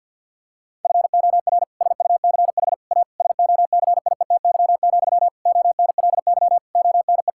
Morse code, heard as a keyed tone, about 50 wpm, 700 Hz. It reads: FOR SUCH A SOCIETY, ONLY ONE
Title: Sentences from Top 1000 Words - Review Part 6 - 50wpm